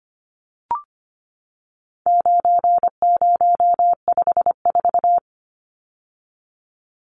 Morse code, heard as a keyed tone, about 25 wpm, 700 Hz. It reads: 9054
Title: Numbers - 4-Digit 25wpm